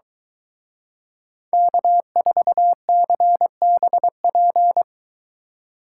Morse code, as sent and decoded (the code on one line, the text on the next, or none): -.- ....- -.-. -... .--.
K4CBP